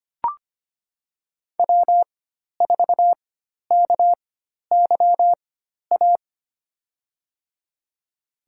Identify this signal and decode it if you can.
Morse code, heard as a keyed tone, about 25 wpm, spaced as for 12 wpm, 700 Hz. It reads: W4KYA
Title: USA Call Signs - 5 Characters - 4x Character Spacing - 25wpm